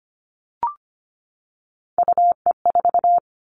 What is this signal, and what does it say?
Morse code, heard as a keyed tone, about 25 wpm, 700 Hz. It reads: UE4
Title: Three Challenging Letter-Number Combinations 25wpm